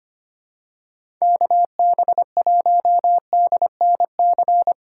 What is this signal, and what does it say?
Morse code, heard as a keyed tone, about 25 wpm, 700 Hz. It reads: KB1DNC
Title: USA Call Signs - 6 Characters - 3x Character Spacing - 25wpm